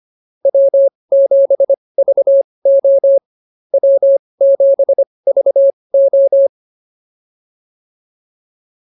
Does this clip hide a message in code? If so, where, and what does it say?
Yes: Morse code from 0.4 to 6.5 s, heard as a keyed tone, about 25 wpm, spaced as for 20 wpm, 550 Hz. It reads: W7VO W7VO